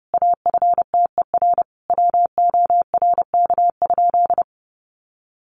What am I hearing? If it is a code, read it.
Morse code, heard as a keyed tone, about 30 wpm, 700 Hz. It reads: AFTER WORK?
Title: Sentences from Top 1400 Words 30wpm